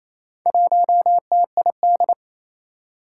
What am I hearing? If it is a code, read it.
Morse code, heard as a keyed tone, about 27 wpm, 700 Hz. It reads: 1TID
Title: Four Challenging Letter-Number Combinations 28wpm